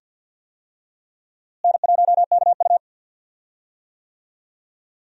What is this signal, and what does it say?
Morse code, heard as a keyed tone, about 52 wpm, 700 Hz. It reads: N1KU
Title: USA Call Signs - 4 Characters - Repeated 2x - 50wpm